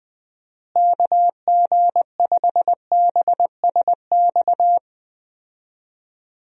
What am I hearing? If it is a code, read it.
Morse code, heard as a keyed tone, about 20 wpm, 700 Hz. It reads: KG5BSX
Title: USA Call Signs - 6 Characters - Repeated 3x - 20wpm